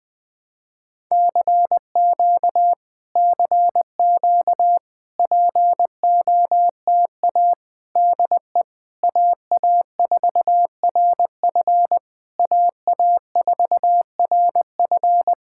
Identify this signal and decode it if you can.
Morse code, heard as a keyed tone, about 20 wpm, 700 Hz. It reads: CQ CQ POTA DE AA4RF AA4RF
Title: POTA QSO Practice - 20wpm